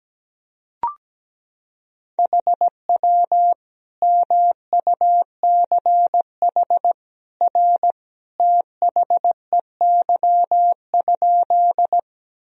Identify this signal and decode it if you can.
Morse code, heard as a keyed tone, about 17 wpm, 700 Hz. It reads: HW MUCH R THEY?
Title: Sentences from Top 200 Words and Abbreviations 17wpm